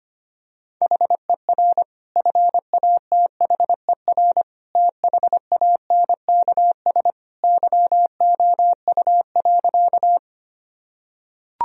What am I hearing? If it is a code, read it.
Morse code, heard as a keyed tone, about 25 wpm, 700 Hz. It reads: HER FATHER THANKS YOU.
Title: Sentences from Top 1000 Words - Review Part 3 - 25wpm